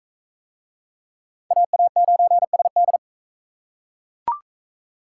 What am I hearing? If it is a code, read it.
Morse code, heard as a keyed tone, about 40 wpm, 700 Hz. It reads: AA9SD